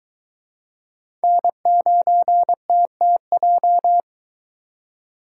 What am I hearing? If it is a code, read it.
Morse code, heard as a keyed tone, about 22 wpm, 700 Hz. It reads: N9TTJ